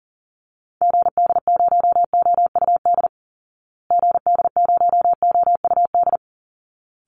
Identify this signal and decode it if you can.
Morse code, heard as a keyed tone, about 40 wpm, 700 Hz. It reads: GD0OUD GD0OUD
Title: Call Signs Repeated 3x - 40wpm